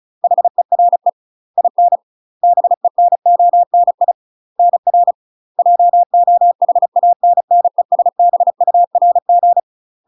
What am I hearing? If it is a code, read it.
Morse code, heard as a keyed tone, about 35 wpm, 700 Hz. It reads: HERE IN BENONI NR JOHANNESBURG
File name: UK MASTER QSOs TRAINING / 35WPM / UK MASTER QSOs 0017